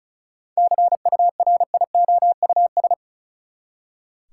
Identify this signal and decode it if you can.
Morse code, heard as a keyed tone, about 34 wpm, 700 Hz. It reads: CURIOUS
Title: Seven Letter Words - 35wpm